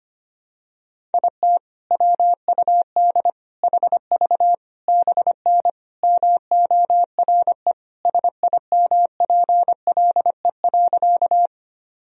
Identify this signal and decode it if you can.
Morse code, heard as a keyed tone, about 25 wpm, 700 Hz. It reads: IT WUD HV BN MORE SIMPLE.
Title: Sentences from Top 400 Words and Abbreviations 25wpm